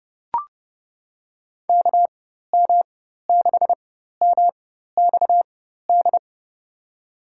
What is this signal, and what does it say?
Morse code, heard as a keyed tone, about 31 wpm, spaced as for 14 wpm, 700 Hz. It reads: KM6MXD
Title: USA Call Signs - 6 Characters - 4x Character Spacing - 30wpm